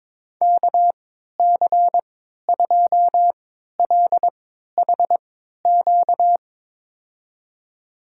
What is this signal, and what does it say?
Morse code, heard as a keyed tone, about 21 wpm, spaced as for 12 wpm, 700 Hz. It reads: KC2LHQ
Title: USA Call Signs - 6 Characters - 3x Character Spacing - 22wpm